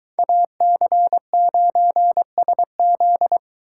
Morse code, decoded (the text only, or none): AC9SZ